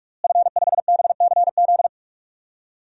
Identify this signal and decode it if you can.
Morse code, heard as a keyed tone, about 46 wpm, 700 Hz. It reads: F5BCZ
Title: DXCC-100 Call Signs - 5 Characters - Repeated 3x - 45wpm